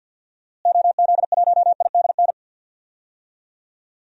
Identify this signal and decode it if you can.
Morse code, heard as a keyed tone, about 52 wpm, 700 Hz. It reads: OZ1IDN
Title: DXCC-100 Call Signs - 6 Characters - Repeated 3x - 50wpm